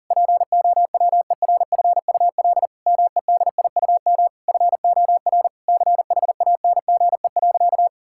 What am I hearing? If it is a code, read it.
Morse code, heard as a keyed tone, about 40 wpm, 700 Hz. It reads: POWERFUL MEDIUM FOR CHANGE.
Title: Sentences from Top 1000 Words - Review Part 6 - 40wpm